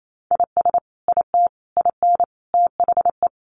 Morse code, heard as a keyed tone, about 28 wpm, 700 Hz. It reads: IS IT IN THE
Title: Sentences from Top 1300 Words 28wpm